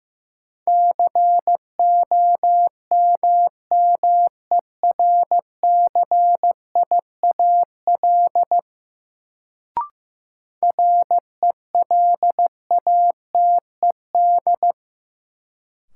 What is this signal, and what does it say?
Morse code, heard as a keyed tone, about 15 wpm, 700 Hz. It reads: COMMERCIAL RELATED
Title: Sentences from Top 1000 Words - Review Part 5 - New Word Drill - 15wpm